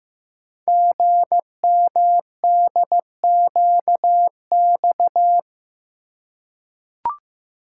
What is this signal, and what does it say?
Morse code, heard as a keyed tone, about 15 wpm, 700 Hz. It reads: GMDQX
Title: Code Groups - 5 Letters - 15wpm